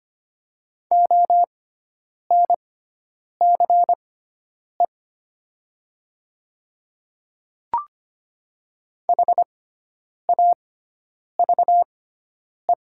Morse code, heard as a keyed tone, about 25 wpm, spaced as for 9 wpm, 700 Hz. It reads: ONCE HAVE
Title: 6x Character Spacing - Top 500 Words - 25wpm